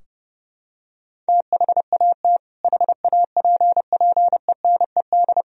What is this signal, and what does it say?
Morse code, heard as a keyed tone, about 32 wpm, 700 Hz. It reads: THAT HAPPENED
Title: Sentences from Top 1000 Words - Review Part 2 - 30wpm